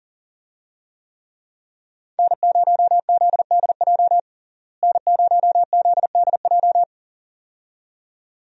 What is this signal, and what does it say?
Morse code, heard as a keyed tone, about 40 wpm, 700 Hz. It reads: N0ZDJ N0ZDJ
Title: USA Call Signs - 5 Characters - Repeated 2x - 40wpm